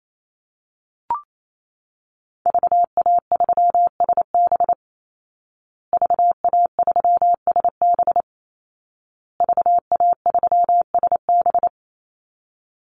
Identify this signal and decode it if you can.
Morse code, heard as a keyed tone, about 28 wpm, 700 Hz. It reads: VA3SB VA3SB VA3SB
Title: Call Signs Repeated 2x - 28wpm